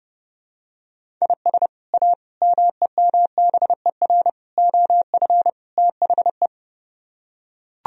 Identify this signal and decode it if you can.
Morse code, heard as a keyed tone, about 30 wpm, 700 Hz. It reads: IS A MEMBER OF THE